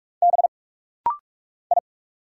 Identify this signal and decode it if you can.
Morse code, heard as a keyed tone, about 46 wpm, 700 Hz. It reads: B I
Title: Single Letters - Rapid-Fire 45wpm